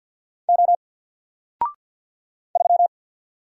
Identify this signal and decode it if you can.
Morse code, heard as a keyed tone, about 50 wpm, 700 Hz. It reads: O 3
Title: Single Letter-Number Alt 50wpm